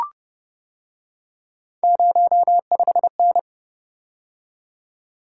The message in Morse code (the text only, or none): 05N